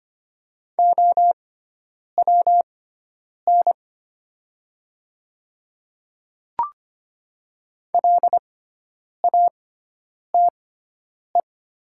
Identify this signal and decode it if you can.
Morse code, heard as a keyed tone, about 25 wpm, spaced as for 9 wpm, 700 Hz. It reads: OWN LATE
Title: 6x Character Spacing - Top 300 Words - 25wpm